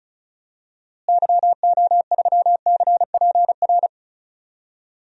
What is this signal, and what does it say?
Morse code, heard as a keyed tone, about 34 wpm, 700 Hz. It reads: YO3CPR
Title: DXCC-100 Call Signs - 6 Characters - Repeated 2x - 35wpm